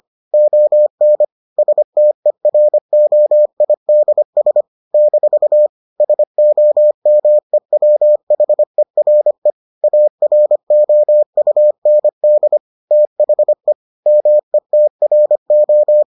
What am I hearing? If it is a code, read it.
Morse code, heard as a keyed tone, about 25 wpm, 600 Hz. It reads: ON STEROIDS - SOMEWHERE AROUND THE METRO